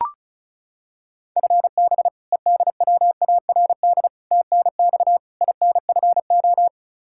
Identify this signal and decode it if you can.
Morse code, heard as a keyed tone, about 36 wpm, 700 Hz. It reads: FB EDWARD TNX INFO
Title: Practice QSO - Exchange 3 - More Info - 35wpm